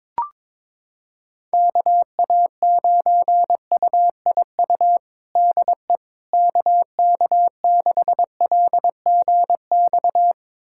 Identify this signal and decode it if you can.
Morse code, heard as a keyed tone, about 22 wpm, 700 Hz. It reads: KA9UIU DE KK6LGX
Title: USA Callsign DE Callsign - 6 Characters - 22wpm